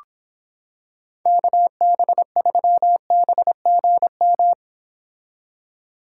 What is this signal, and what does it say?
Morse code, heard as a keyed tone, about 25 wpm, 700 Hz. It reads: KB3BGM